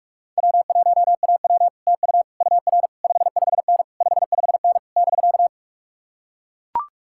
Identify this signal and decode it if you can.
Morse code, heard as a keyed tone, about 44 wpm, 700 Hz. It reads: W1AW TU UR 55N 55N <BK>